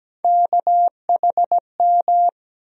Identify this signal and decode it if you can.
Morse code, heard as a keyed tone, about 17 wpm, 700 Hz. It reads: KHM
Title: USA Call Sign Postfixes - 3 Letters 17wpm